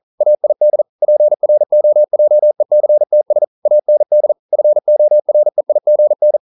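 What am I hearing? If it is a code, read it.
Morse code, heard as a keyed tone, about 42 wpm, 600 Hz. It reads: AID PROJECTS AND FOREIGN
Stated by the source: Morse code practice at 41 wpm, from CW med Gnister.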